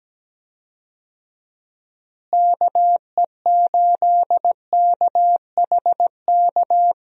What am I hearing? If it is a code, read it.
Morse code, heard as a keyed tone, about 17 wpm, 700 Hz. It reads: KE8KHK
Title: USA Call Signs - 6 Characters - Repeated 3x - 17wpm